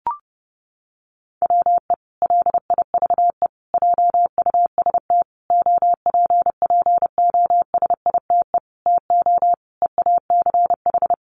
Morse code, heard as a keyed tone, about 30 wpm, 700 Hz. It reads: WE LIVE JUST OPPOSITE TO EACH
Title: Sentences from Top 1300 Words 30wpm